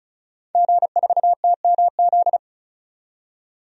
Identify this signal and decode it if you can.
Morse code, heard as a keyed tone, about 34 wpm, 700 Hz. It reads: G4TMZ
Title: DXCC-100 Call Signs - 5 Characters - 35wpm